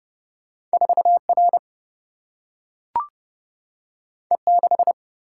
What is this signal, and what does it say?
Morse code, heard as a keyed tone, about 30 wpm, 700 Hz. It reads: 4R E6
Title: Single-Letter Single-Number 30wpm